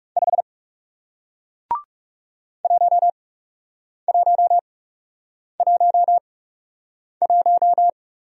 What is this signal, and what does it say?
Morse code, heard as a keyed tone, about 40 wpm, spaced as for 8 wpm, 700 Hz. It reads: 5 1111